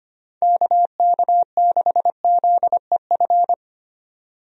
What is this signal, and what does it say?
Morse code, heard as a keyed tone, about 25 wpm, 700 Hz. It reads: KK6ZEF
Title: USA Call Signs - 6 Characters - Repeated 3x - 25wpm